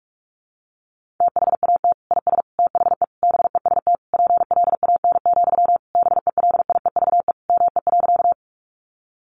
Morse code, heard as a keyed tone, about 44 wpm, 700 Hz. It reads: THAT IS THE BEST PLAN, BELIEVE ME.
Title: Sentences from Top 1000 Words 45wpm